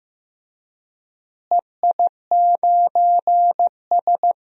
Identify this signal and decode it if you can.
Morse code, heard as a keyed tone, about 15 wpm, 700 Hz. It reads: EI9S
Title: DXCC-100 Call Signs - 4 Characters - Repeated 3x - 15wpm